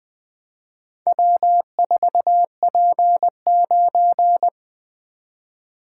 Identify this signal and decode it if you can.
Morse code, heard as a keyed tone, about 20 wpm, 700 Hz. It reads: W4P9
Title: Four Letter-Number Combinations - 20wpm